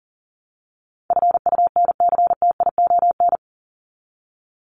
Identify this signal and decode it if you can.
Morse code, heard as a keyed tone, about 40 wpm, 700 Hz. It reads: FUNCTION
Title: ES:S Suffix - Top 100 Words with Base Words 40wpm